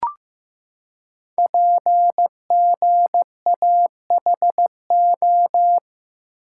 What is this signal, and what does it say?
Morse code, heard as a keyed tone, about 15 wpm, 700 Hz. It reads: PGAHO
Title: Code Groups - 5 Letters - 15wpm